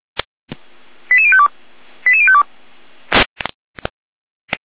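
Five-tone selective call (ZVEI), heard as quick runs of five tones, tones of about 70 ms. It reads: 88052 88052
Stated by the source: Received by SDR.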